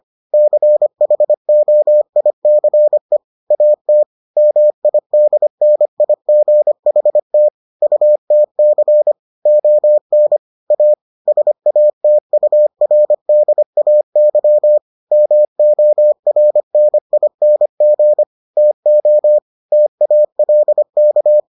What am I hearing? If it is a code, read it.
Morse code, heard as a keyed tone, about 25 wpm, 600 Hz. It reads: CHOICE AT MIDNIGHT UTC ON A SATURDAY MORNING TO TALK